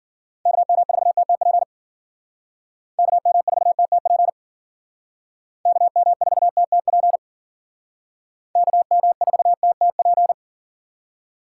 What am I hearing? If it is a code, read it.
Morse code, heard as a keyed tone, about 51 wpm, 700 Hz. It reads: KM4TTP KM4TTP KM4TTP KM4TTP